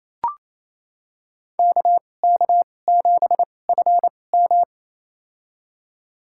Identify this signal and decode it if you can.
Morse code, heard as a keyed tone, about 28 wpm, spaced as for 20 wpm, 700 Hz. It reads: KK7FM
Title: USA Call Signs - 5 Characters - 2x Character Spacing - 28wpm